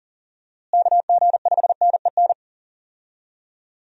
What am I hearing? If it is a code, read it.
Morse code, heard as a keyed tone, about 37 wpm, 700 Hz. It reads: KG5NEN
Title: USA Call Signs - 6 Characters - Repeated 2x - 40wpm